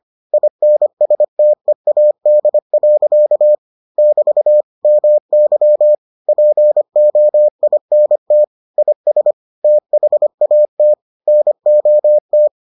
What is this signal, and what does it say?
Morse code, heard as a keyed tone, about 25 wpm, 600 Hz. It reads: INSTEAD. <BT> MY POINT IS THAT NOT